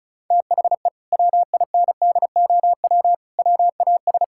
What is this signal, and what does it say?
Morse code, heard as a keyed tone, about 36 wpm, 700 Hz. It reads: THE WINDOW WAS
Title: Sentences from Top 1600 Words 35wpm